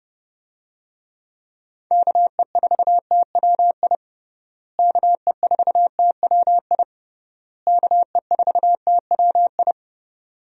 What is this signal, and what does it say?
Morse code, heard as a keyed tone, about 30 wpm, 700 Hz. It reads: KE4TWI KE4TWI KE4TWI
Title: USA Call Signs - 6 Characters - Repeated 3x - 30wpm